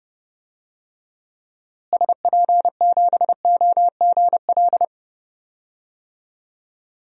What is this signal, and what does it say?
Morse code, heard as a keyed tone, about 32 wpm, 700 Hz. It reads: SP7OGL
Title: DXCC-100 Call Signs - 6 Characters - Repeated 2x - 30wpm